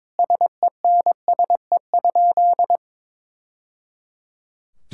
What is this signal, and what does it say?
Morse code, heard as a keyed tone, about 22 wpm, 700 Hz. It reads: SENSE?
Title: Sentences from Top 1000 Words - Review Part 5 - 22wpm